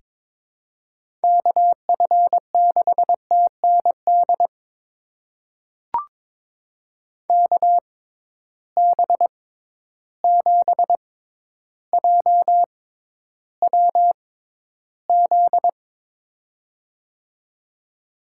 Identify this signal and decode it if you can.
Morse code, heard as a keyed tone, about 22 wpm, spaced as for 11 wpm, 700 Hz. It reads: KF6TND KB7JWZ
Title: USA Call Signs - 6 Characters - 6x Character Spacing - 22wpm